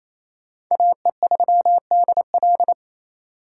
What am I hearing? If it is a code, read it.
Morse code, heard as a keyed tone, about 34 wpm, 700 Hz. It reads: AE3DL
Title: Speed Racing - USA 2x2 Call Signs 28wpm